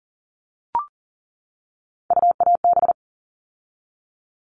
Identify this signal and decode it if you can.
Morse code, heard as a keyed tone, about 40 wpm, 700 Hz. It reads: UAB
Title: All Cut Numbers - 3-Digit 40wpm